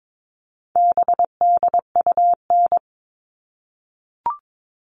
Morse code, heard as a keyed tone, about 22 wpm, 700 Hz. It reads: BDUN
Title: All Cut Numbers - 4-Digit 22wpm